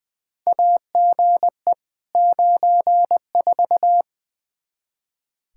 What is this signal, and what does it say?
Morse code, heard as a keyed tone, about 20 wpm, 700 Hz. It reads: AGE 94
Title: Practice QSO - Exchange 3 - More Info - 20wpm